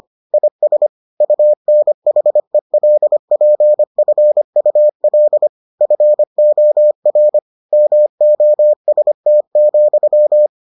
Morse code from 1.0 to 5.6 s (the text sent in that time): UNHELPFUL